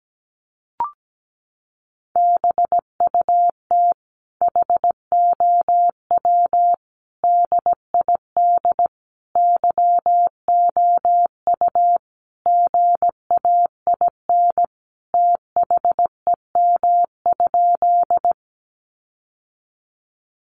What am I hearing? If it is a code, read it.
Morse code, heard as a keyed tone, about 17 wpm, 700 Hz. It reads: BUT HOW DID YOU GAIN THEM?
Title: Sentences from Top 1000 Words 17wpm